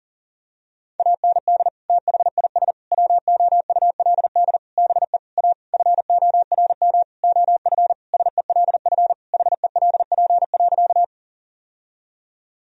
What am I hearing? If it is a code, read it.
Morse code, heard as a keyed tone, about 40 wpm, 700 Hz. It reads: AND THIS WOULD BE A FORM OF SELF HELP.
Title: Sentences from Top 1000 Words - Review Part 1 - 40wpm